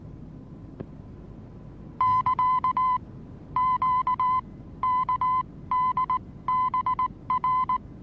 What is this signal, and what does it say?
Morse code, heard as a keyed tone, about 19 wpm, spaced as for 15 wpm, 1010 Hz. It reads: <KA> QKDBR
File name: repos / Morsekurs / DJ4UF Morsen26